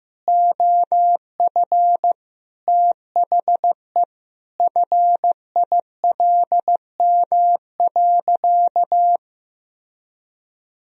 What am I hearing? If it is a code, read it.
Morse code, heard as a keyed tone, about 15 wpm, 700 Hz. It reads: OF THE FILM.
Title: Sentences from Top 1000 Words - Review Part 3 - 15wpm